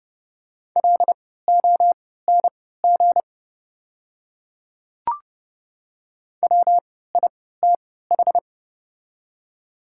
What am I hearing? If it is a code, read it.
Morse code, heard as a keyed tone, about 30 wpm, spaced as for 17 wpm, 700 Hz. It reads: LONG WITH
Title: 3x Character Spacing - Top 100 Words - 30wpm